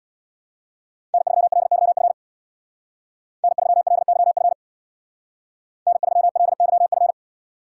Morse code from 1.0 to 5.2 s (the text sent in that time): N3LYF N3LYF